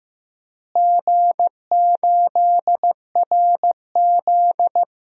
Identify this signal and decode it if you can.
Morse code, heard as a keyed tone, about 15 wpm, 700 Hz. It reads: G8RZ
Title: DXCC-100 Call Signs - 4 Characters - Repeated 3x - 15wpm